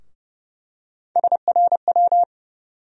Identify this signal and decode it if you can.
Morse code, heard as a keyed tone, about 30 wpm, 700 Hz. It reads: SRW
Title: USA Call Sign Postfixes - 3 Letters 30wpm